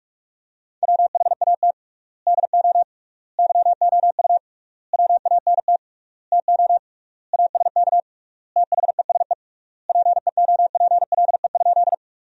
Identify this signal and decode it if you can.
Morse code, heard as a keyed tone, about 41 wpm, 700 Hz. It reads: WHAT DO YOU WANT TO ASK THESE PEOPLE?
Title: 3x Word-Spacing - Sentences from Top 200 Words - 45wpm